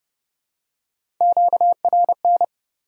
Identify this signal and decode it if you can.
Morse code, heard as a keyed tone, about 28 wpm, 700 Hz. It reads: QRN